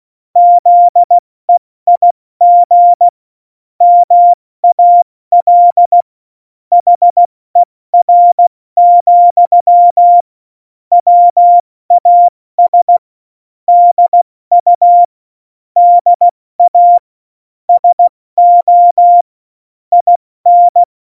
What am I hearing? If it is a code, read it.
Morse code, heard as a keyed tone, about 16 wpm, 700 Hz. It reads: ZEIG MAL HER, WAS DU DA SO IN
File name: Morsekurs / Witze / Witz 17.Juli 2018, 70 F80